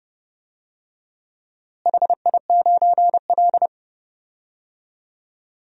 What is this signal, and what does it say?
Morse code, heard as a keyed tone, about 30 wpm, 700 Hz. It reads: HI9L
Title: DXCC-100 Call Signs - 4 Characters - Repeated 2x - 30wpm